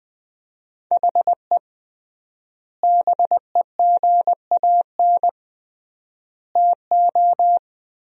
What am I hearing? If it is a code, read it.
Morse code, heard as a keyed tone, about 20 wpm, 700 Hz. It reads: HE BEGAN TO
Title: 3x Word-Spacing - Sentences from Top 500 Words - 20wpm